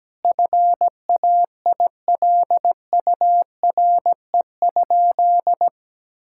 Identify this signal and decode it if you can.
Morse code, heard as a keyed tone, about 17 wpm, 700 Hz. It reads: FAILURE?
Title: Sentences from Top 1000 Words - Review Part 6 - 17wpm